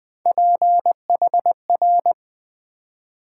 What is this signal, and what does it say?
Morse code, heard as a keyed tone, about 20 wpm, 700 Hz. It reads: PHR
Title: Build A Word - 3 Letters At A Time - 6-Letter Words - 20wpm